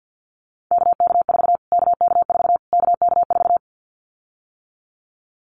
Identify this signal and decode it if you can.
Morse code, heard as a keyed tone, about 51 wpm, 700 Hz. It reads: KK4 KK4 KK4
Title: Callsign Prefixes for Top 25 DXCC - Repeated 3x - 50wpm